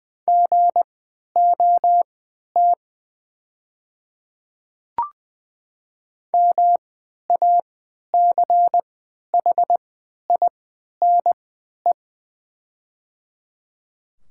Morse code, heard as a keyed tone, about 20 wpm, spaced as for 11 wpm, 690 Hz. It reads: GOT MACHINE
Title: 3x Character Spacing - Top 500 Words - 20wpm